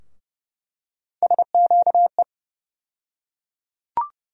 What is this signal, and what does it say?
Morse code, heard as a keyed tone, about 32 wpm, 700 Hz. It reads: SQE
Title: USA Call Sign Postfixes - 3 Letters 30wpm